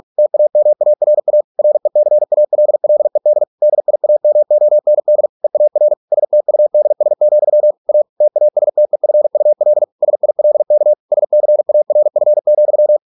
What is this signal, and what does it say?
Morse code, heard as a keyed tone, about 46 wpm, 600 Hz. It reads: TAMARA RECALLED DIAMOND EAR STUDS, A TASTEFUL SILK SCARF,